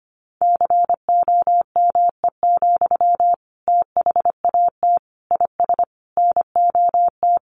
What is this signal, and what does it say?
Morse code, heard as a keyed tone, about 25 wpm, 700 Hz. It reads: COME, THAT IS NOT
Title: Sentences from Top 700 Words 25wpm